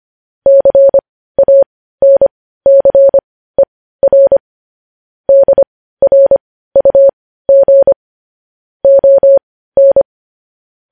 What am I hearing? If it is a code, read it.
Morse code, heard as a keyed tone, about 25 wpm, spaced as for 15 wpm, 550 Hz. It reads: CANCER DRUG ON
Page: news-headlines-in-morse-code-at-15-wpm